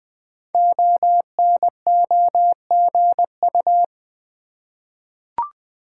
Morse code, heard as a keyed tone, about 20 wpm, 700 Hz. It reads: ONOGU
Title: Code Groups - 5 Letters - 20wpm